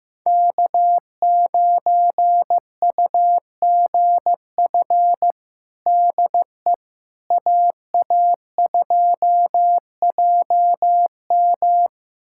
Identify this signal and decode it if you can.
Morse code, heard as a keyed tone, about 15 wpm, 700 Hz. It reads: K9UGF DE AA2JM
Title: USA Callsign DE Callsign - 5 Characters - 15wpm